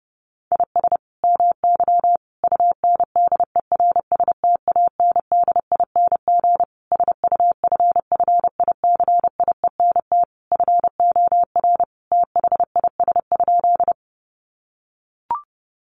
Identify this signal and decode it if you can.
Morse code, heard as a keyed tone, about 30 wpm, 700 Hz. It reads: IS MY UNDERSTANDING SUFFICIENT FOR THIS?
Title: Sentences from Top 1400 Words 30wpm